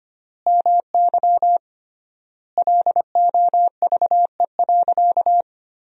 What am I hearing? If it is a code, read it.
Morse code, heard as a keyed tone, about 25 wpm, 700 Hz. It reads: MY LOVE.
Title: 3x Word-Spacing - Sentences from Top 400 Words - 25wpm